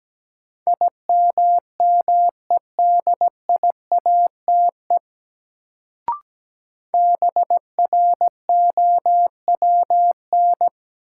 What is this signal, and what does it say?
Morse code, heard as a keyed tone, about 17 wpm, 700 Hz. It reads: IMMEDIATE BROWN
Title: Sentences from Top 1200 Words - New Word Drill - 17wpm